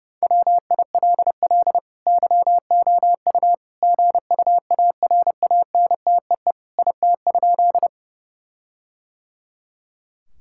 Morse code, heard as a keyed tone, about 32 wpm, 700 Hz. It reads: WILL YOU GUARANTEE IT?
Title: Sentences from Top 1600 Words 30wpm